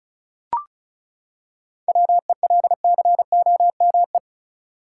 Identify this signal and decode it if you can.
Morse code, heard as a keyed tone, about 34 wpm, 700 Hz. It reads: WELCOME
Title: Seven Letter Words - 35wpm